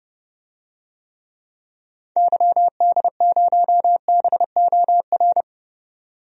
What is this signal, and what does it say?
Morse code, heard as a keyed tone, about 30 wpm, 700 Hz. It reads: YD0BOR
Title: DXCC-100 Call Signs - 6 Characters - 30wpm